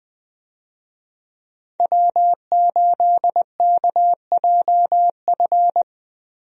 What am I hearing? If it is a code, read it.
Morse code, heard as a keyed tone, about 20 wpm, 700 Hz. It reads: W8KJF
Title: USA Call Signs - 5 Characters - Repeated 3x - 20wpm